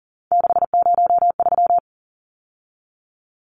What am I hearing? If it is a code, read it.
Morse code, heard as a keyed tone, about 40 wpm, 700 Hz. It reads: 603
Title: Numbers - 3-Digit 40wpm